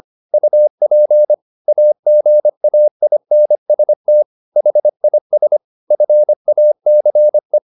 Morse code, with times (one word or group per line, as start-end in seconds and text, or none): 0.3-1.3 UP
1.7-4.2 AGAINST
4.6-5.6 HIS
5.9-7.6 FACE